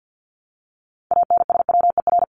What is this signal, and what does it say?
Morse code, heard as a keyed tone, about 50 wpm, 700 Hz. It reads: ANSWER
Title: Top 300 Words Repeated 50wpm